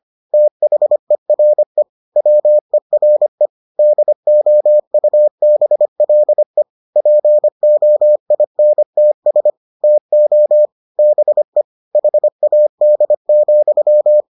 Morse code, heard as a keyed tone, about 25 wpm, 600 Hz. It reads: THERE WERE DOUBLE POINTS TO BE HAD,